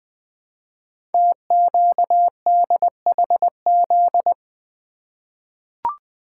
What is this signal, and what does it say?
Morse code, heard as a keyed tone, about 20 wpm, 700 Hz. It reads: TQDHZ
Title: Code Groups - 5 Letter:Numbers - 20wpm